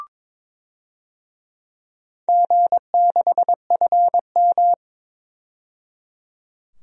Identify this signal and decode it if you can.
Morse code, heard as a keyed tone, about 22 wpm, 700 Hz. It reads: G6FM
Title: DXCC-100 Call Signs - 4 Characters - 22wpm